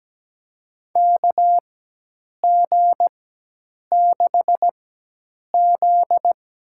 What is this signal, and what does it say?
Morse code, heard as a keyed tone, about 17 wpm, spaced as for 8 wpm, 700 Hz. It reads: KG6Z
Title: USA Call Signs - 4x Character Spacing - 17wpm